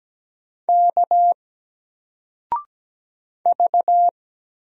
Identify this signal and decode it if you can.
Morse code, heard as a keyed tone, about 17 wpm, 700 Hz. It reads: K V
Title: Single Letter-Number Alt 17wpm